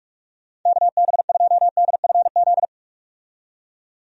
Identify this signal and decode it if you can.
Morse code, heard as a keyed tone, about 46 wpm, 700 Hz. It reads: KB2DFZ
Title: USA Call Signs - 6 Characters - 45wpm